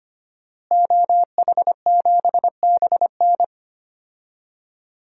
Morse code, heard as a keyed tone, about 25 wpm, 700 Hz. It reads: OH7BN
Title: DXCC-100 Call Signs - 5 Characters - Repeated 2x - 25wpm